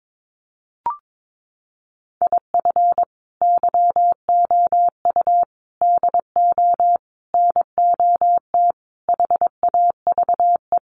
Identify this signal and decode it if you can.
Morse code, heard as a keyed tone, about 22 wpm, 700 Hz. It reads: IF YOU DO NOT HAVE
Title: Sets of 5 Words 22wpm